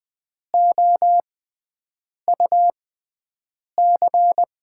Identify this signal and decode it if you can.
Morse code, heard as a keyed tone, about 20 wpm, spaced as for 7 wpm, 700 Hz. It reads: OUC
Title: USA Call Signs - 5 Characters - 6x Character Spacing - 20wpm